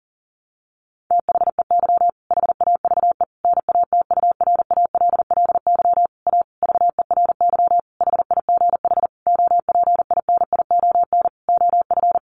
Morse code, heard as a keyed tone, about 40 wpm, 700 Hz. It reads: THEY HAVE NATURALLY A VERY HIGH OPINION OF
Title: Sentences from Top 1300 Words 40wpm